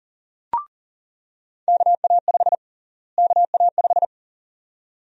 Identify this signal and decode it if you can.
Morse code, heard as a keyed tone, about 40 wpm, 700 Hz. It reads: KA5 KA5
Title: USA Call Sign Prefixes - Repeated 2x - 40wpm